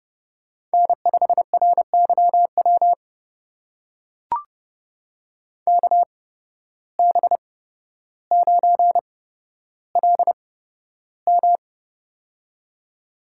Morse code, heard as a keyed tone, about 30 wpm, spaced as for 13 wpm, 700 Hz. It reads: N5RYW KB9LM
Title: USA Call Signs - 5 Characters - 8x Character Spacing - 30wpm